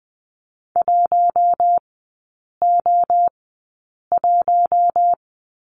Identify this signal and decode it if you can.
Morse code, heard as a keyed tone, about 20 wpm, spaced as for 8 wpm, 700 Hz. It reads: 1O1
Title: 041 Lesson 8e - Number 1 - Three Characters 20wpm